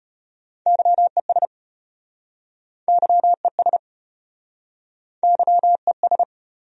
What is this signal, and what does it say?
Morse code, heard as a keyed tone, about 33 wpm, 700 Hz. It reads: YES YES YES